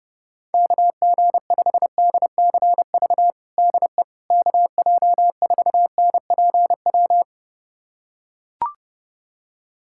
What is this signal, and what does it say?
Morse code, heard as a keyed tone, about 30 wpm, 700 Hz. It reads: KG5DCV DE KJ4NPW
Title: USA Callsign DE Callsign - 6 Characters - 30wpm